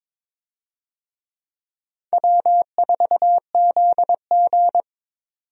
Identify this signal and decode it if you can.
Morse code, heard as a keyed tone, about 22 wpm, 700 Hz. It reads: W4ZG
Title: USA Call Signs - 4 Characters - 22wpm